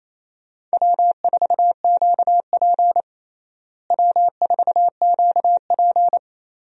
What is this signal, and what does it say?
Morse code, heard as a keyed tone, about 28 wpm, 700 Hz. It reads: W4QP W4QP
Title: USA Call Signs - 4 Characters - Repeated 3x - 28wpm